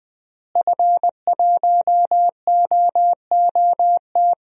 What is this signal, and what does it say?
Morse code, heard as a keyed tone, about 20 wpm, 700 Hz. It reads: F1OOT
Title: DXCC-100 Call Signs - 5 Characters - Repeated 2x - 20wpm